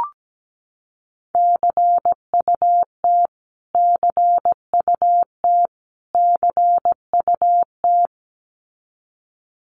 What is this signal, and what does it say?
Morse code, heard as a keyed tone, about 17 wpm, 700 Hz. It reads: CUT CUT CUT
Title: Top 500 Words Repeated 17wpm